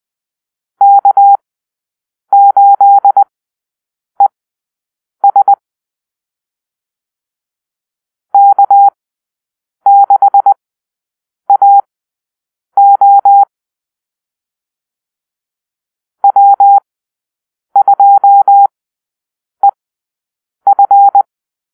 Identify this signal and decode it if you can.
Morse code, heard as a keyed tone, about 20 wpm, spaced as for 7 wpm, 800 Hz. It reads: K8ES K6AO W2EF
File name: arrl / Call-Sign Drill-10